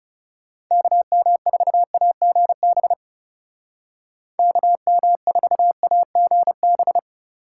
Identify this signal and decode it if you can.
Morse code, heard as a keyed tone, about 36 wpm, 700 Hz. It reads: KM4AGB KM4AGB
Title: Speed Racing - USA 2x3 Call Signs 30wpm